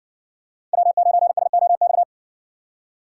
Morse code, heard as a keyed tone, about 60 wpm, 700 Hz. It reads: W9SOX